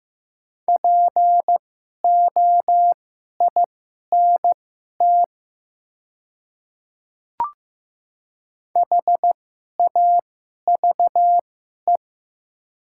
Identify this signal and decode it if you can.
Morse code, heard as a keyed tone, about 15 wpm, spaced as for 11 wpm, 700 Hz. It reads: POINT HAVE
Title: 2x Character Spacing - Top 300 Words - 15wpm